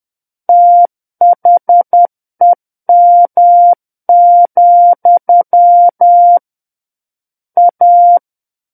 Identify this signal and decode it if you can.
Morse code, heard as a keyed tone, about 10 wpm, 700 Hz. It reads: THEM, A
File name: Book One / 01-Chapter 1 - 10 WPM